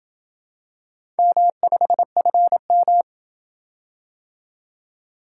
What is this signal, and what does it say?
Morse code, heard as a keyed tone, about 27 wpm, 700 Hz. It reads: M5FM